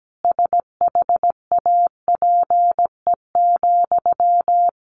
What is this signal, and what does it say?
Morse code, heard as a keyed tone, about 17 wpm, 700 Hz. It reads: SHAPE,
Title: Sentences from Top 1400 Words 17wpm